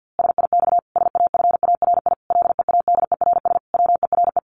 Morse code, heard as a keyed tone, about 50 wpm, 700 Hz. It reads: SIX SAFARI LEADERS WERE
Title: A Sound Of Thunder 50wpm